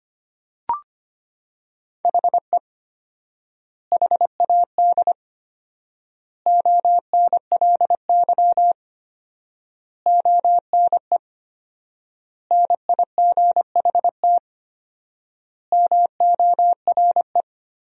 Morse code, heard as a keyed tone, about 25 wpm, 700 Hz. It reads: HE HAD ONLY ONE NIGHT MORE
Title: 4x Word-Spacing - Sentences from Top 300 Words - 25wpm